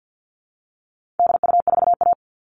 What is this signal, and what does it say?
Morse code, heard as a keyed tone, about 50 wpm, 700 Hz. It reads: DU4A
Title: All Cut Numbers - 4-Digit 50wpm